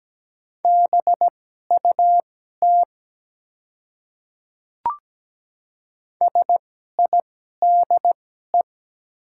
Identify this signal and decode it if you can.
Morse code, heard as a keyed tone, about 17 wpm, spaced as for 12 wpm, 700 Hz. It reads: BUT SIDE
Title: 2x Character Spacing - Top 100 Words - 17wpm